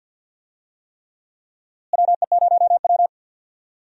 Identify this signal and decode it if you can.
Morse code, heard as a keyed tone, about 52 wpm, 700 Hz. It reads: WE0W
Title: USA Call Signs - 4 Characters - Repeated 3x - 50wpm